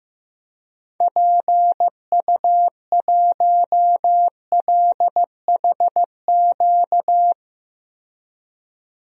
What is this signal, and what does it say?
Morse code, heard as a keyed tone, about 15 wpm, 700 Hz. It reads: PU1LHQ
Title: DXCC-100 Call Signs - 6 Characters - 15wpm